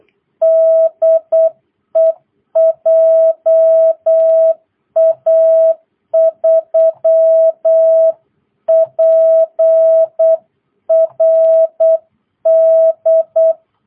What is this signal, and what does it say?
Morse code, heard as a keyed tone, about 8 wpm, 660 Hz. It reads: DE JA3PRD